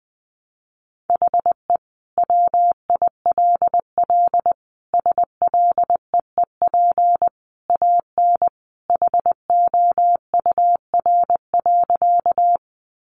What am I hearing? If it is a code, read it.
Morse code, heard as a keyed tone, about 19 wpm, 700 Hz. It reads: HE WILL SLEEP AN HOUR.
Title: Sentences from Top 500 Words 20wpm